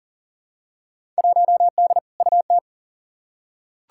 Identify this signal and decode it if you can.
Morse code, heard as a keyed tone, about 40 wpm, 700 Hz. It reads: 1D UT
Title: Field Day Exchanges - 40wpm